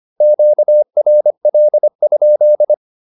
Morse code, heard as a keyed tone, about 25 wpm, 600 Hz. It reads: QRL?
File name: QSO103 25